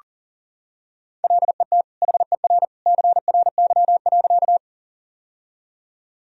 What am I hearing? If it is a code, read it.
Morse code, heard as a keyed tone, about 40 wpm, 700 Hz. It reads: LET HER CRY.
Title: Sentences from Top 1000 Words - Review Part 2 - 40wpm